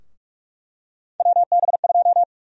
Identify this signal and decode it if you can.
Morse code, heard as a keyed tone, about 48 wpm, 700 Hz. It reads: WB2
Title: USA Call Sign Prefixes - Repeated 2x - 45wpm